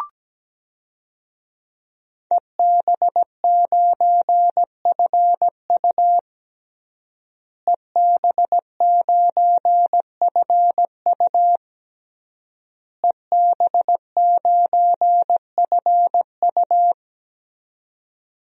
Morse code, heard as a keyed tone, about 17 wpm, 700 Hz. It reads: EB9FU EB9FU EB9FU
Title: DXCC-100 Call Signs - 5 Characters - Repeated 3x - 17wpm